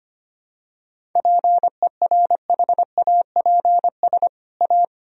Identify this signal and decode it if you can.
Morse code, heard as a keyed tone, about 25 wpm, 700 Hz. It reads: PERHAPS A
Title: Sentences from Top 1000 Words - Review Part 1 - 25wpm